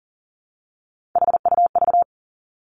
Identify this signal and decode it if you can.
Morse code, heard as a keyed tone, about 38 wpm, 700 Hz. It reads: HUV
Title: Three Challenging Letter-Number Combinations 40wpm